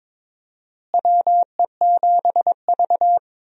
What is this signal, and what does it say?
Morse code, heard as a keyed tone, about 22 wpm, 700 Hz. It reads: WE7V
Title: USA Call Signs - 4 Characters - Repeated 2x - 22wpm